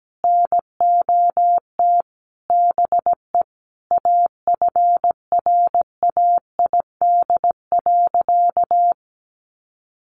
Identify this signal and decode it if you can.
Morse code, heard as a keyed tone, about 17 wpm, 700 Hz. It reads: NOT BE AFRAID.
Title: Sentences from Top 1500 Words 17wpm